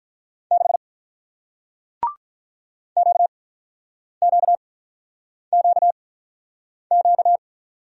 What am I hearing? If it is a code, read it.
Morse code, heard as a keyed tone, about 48 wpm, spaced as for 9 wpm, 700 Hz. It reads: 6 QQQQ